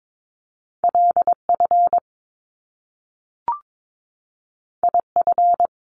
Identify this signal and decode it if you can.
Morse code, heard as a keyed tone, about 22 wpm, 700 Hz. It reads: LF IF
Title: Top Two Letter Combinations in Words 22wpm